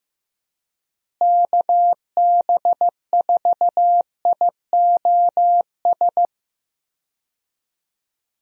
Morse code, heard as a keyed tone, about 16 wpm, 700 Hz. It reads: KB4IOS
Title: USA Call Signs - 6 Characters - Repeated 2x - 15wpm